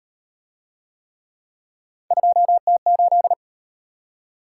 Morse code, heard as a keyed tone, about 37 wpm, 700 Hz. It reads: 2T8